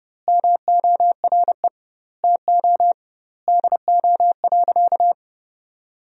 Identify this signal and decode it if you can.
Morse code, heard as a keyed tone, about 30 wpm, 700 Hz. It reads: MORE TO DO.
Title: 2x Word-Spacing - Sentences from Top 100 Words - 30wpm